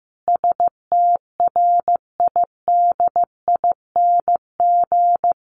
Morse code, heard as a keyed tone, about 14 wpm, 700 Hz. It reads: STRIDING
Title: A Sound Of Thunder 15wpm